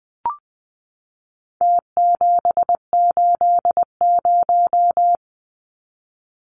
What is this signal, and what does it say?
Morse code, heard as a keyed tone, about 20 wpm, 700 Hz. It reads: T780
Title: Numbers - Contest Style - 4-Digit 20wpm